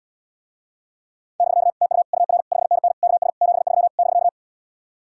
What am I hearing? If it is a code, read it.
Morse code, heard as a keyed tone, about 80 wpm, 700 Hz. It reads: <BK> TU UR 5NN KS 73 <BK>